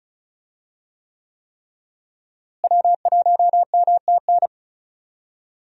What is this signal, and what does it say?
Morse code, heard as a keyed tone, about 36 wpm, 700 Hz. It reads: W1MTN
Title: USA Call Signs - 5 Characters - Repeated 3x - 35wpm